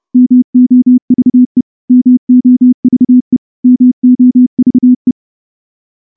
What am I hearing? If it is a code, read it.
Morse code, heard as a keyed tone, about 30 wpm, 250 Hz. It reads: MOVE MOVE MOVE
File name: morse-training-30WPM